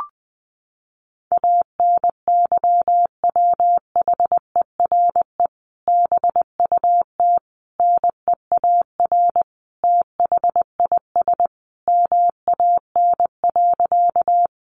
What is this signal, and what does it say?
Morse code, heard as a keyed tone, about 20 wpm, 700 Hz. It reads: ANYWHERE BUT NEAR THIS MAN.
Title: Sentences from Top 1400 Words 20wpm